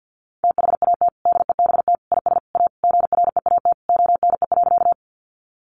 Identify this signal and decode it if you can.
Morse code, heard as a keyed tone, about 50 wpm, 700 Hz. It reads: THAT DEBT IS A GREAT ONE.
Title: Sentences from Top 1300 Words 50wpm